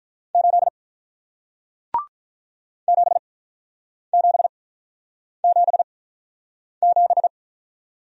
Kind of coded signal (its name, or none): Morse code